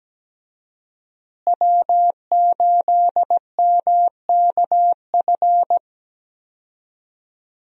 Morse code, heard as a keyed tone, about 17 wpm, 700 Hz. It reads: W8MKF
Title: USA Call Signs - 5 Characters - Repeated 2x - 17wpm